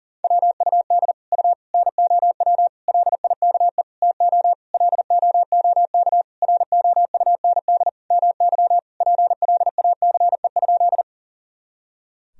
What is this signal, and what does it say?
Morse code, heard as a keyed tone, about 40 wpm, 700 Hz. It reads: WUD U NOW LIKE TO LOOK ROUND MY PLACE?
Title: Sentences from Top 200 Words and Abbreviations 40wpm